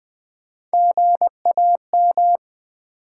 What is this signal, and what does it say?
Morse code, heard as a keyed tone, about 20 wpm, 700 Hz. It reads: GAM
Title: Build A Word - 1 Letter At A Time - 4-Letter Words - 20wpm